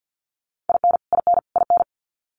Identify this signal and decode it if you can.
Morse code, heard as a keyed tone, about 50 wpm, 700 Hz. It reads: IN IN IN
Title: Top 300 Words Repeated 50wpm